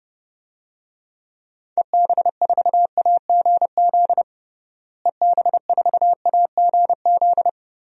Morse code, heard as a keyed tone, about 30 wpm, 700 Hz. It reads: EB4AGZ EB4AGZ
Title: DXCC-100 Call Signs - 6 Characters - Repeated 3x - 30wpm